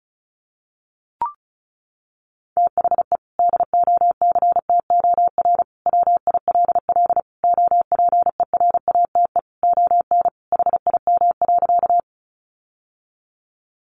Morse code, heard as a keyed tone, about 34 wpm, 700 Hz. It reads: THE DOCTOR WILL OPERATE ON HIM.
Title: Sentences from Top 1500 Words 35wpm